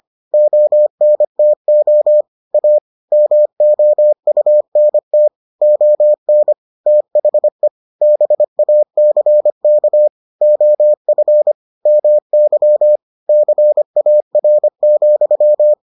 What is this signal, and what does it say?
Morse code, heard as a keyed tone, about 25 wpm, 600 Hz. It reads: ONTO A MOUNT ON THE BACK OF MY CAR,